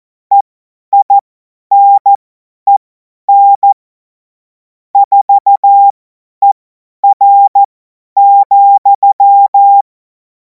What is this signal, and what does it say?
Morse code, heard as a keyed tone, about 14 wpm, spaced as for 10 wpm, 800 Hz. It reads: EINEN 4ER,
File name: Morsekurs / Witze / Witz 31. März 2026 50 F70